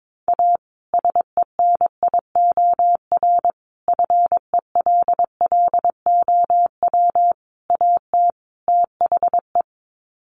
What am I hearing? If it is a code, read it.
Morse code, heard as a keyed tone, about 22 wpm, 700 Hz. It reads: A SENIOR FELLOW AT THE
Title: Sets of 5 Words Encore 22wpm